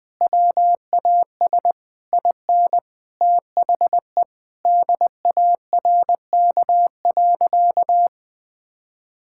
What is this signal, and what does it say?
Morse code, heard as a keyed tone, about 20 wpm, 700 Hz. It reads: WAS IN THE DARK.
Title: Sentences from Top 1600 Words 20wpm